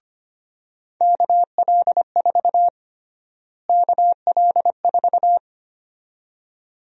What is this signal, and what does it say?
Morse code, heard as a keyed tone, about 25 wpm, 700 Hz. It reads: KL4 KL4
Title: USA Call Sign Prefixes - Repeated 2x - 25wpm